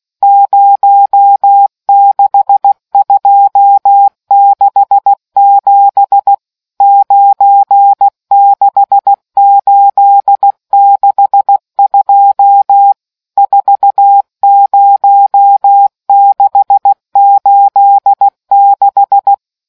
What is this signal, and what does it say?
Morse code, heard as a keyed tone, about 16 wpm, 800 Hz. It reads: 06267 96862 40686